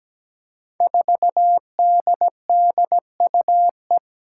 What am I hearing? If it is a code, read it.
Morse code, heard as a keyed tone, about 17 wpm, 700 Hz. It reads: 4DDUE
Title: Five Challenging Letter-Number Combinations 17wpm